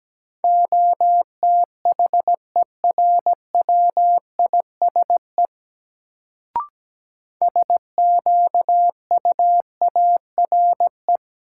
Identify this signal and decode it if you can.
Morse code, heard as a keyed tone, about 17 wpm, 700 Hz. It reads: OTHERWISE SQUARE
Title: Sentences from Top 1200 Words - New Word Drill Alt - 17wpm